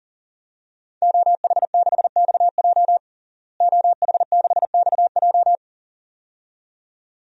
Morse code, heard as a keyed tone, about 40 wpm, 700 Hz. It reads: OH6XJ OH6XJ
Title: DXCC-100 Call Signs - 5 Characters - Repeated 2x - 40wpm